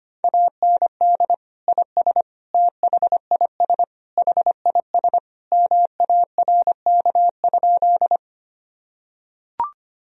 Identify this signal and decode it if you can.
Morse code, heard as a keyed tone, about 25 wpm, 700 Hz. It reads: AND IS THIS HIS MARK?
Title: Sentences from Top 1000 Words - Review Part 1 - 25wpm